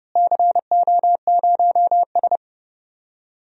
Morse code, heard as a keyed tone, about 30 wpm, 700 Hz. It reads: CO0S
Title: DXCC-100 Call Signs - 4 Characters - Repeated 3x - 30wpm